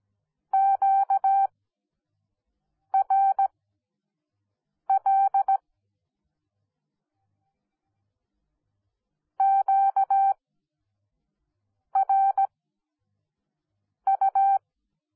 Morse code, heard as a keyed tone, about 17 wpm, spaced as for 5 wpm, 780 Hz. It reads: QRL QRU
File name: Onlinelehrgang / morsen / DJ4UF Morsen10